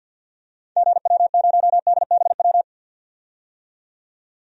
Morse code, heard as a keyed tone, about 51 wpm, 700 Hz. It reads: GW0DBW
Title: DXCC-100 Call Signs - 6 Characters - 50wpm